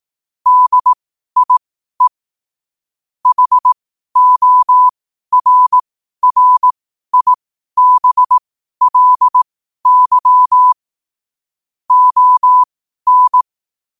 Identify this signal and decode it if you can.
Morse code, heard as a keyed tone, about 18 wpm, spaced as for 12 wpm, 1000 Hz. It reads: DIE HORRIBLY ON
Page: qotd-at-12-wpm